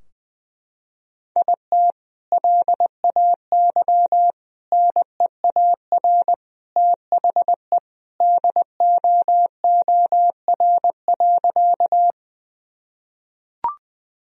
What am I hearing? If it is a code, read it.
Morse code, heard as a keyed tone, about 20 wpm, 700 Hz. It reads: IT LAY NEAR THE DOOR.
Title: Sentences from Top 1000 Words - Review Part 2 - 20wpm